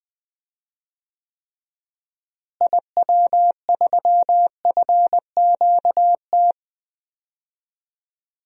Morse code, heard as a keyed tone, about 21 wpm, 700 Hz. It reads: IW3FQT
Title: DXCC-100 Call Signs - 6 Characters - Repeated 3x - 20wpm